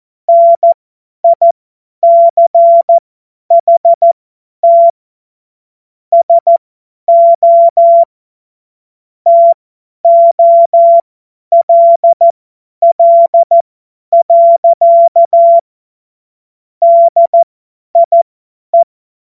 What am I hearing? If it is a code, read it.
Morse code, heard as a keyed tone, about 14 wpm, spaced as for 10 wpm, 680 Hz. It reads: NICHT SO TOLL. DIE
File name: Morsekurs / Witze / Witz 28.November 2023 50 F70